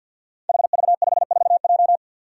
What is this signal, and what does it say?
Morse code, heard as a keyed tone, about 51 wpm, 700 Hz. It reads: HV54J
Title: Five Challenging Letter-Number Combinations 50wpm